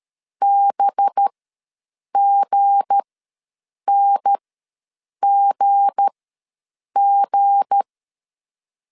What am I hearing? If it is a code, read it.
Morse code, heard as a keyed tone, about 13 wpm, spaced as for 7 wpm, 800 Hz. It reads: BGNGG
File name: LEZIONI / 7 GF9Z / 7 G ws 12wpm 3140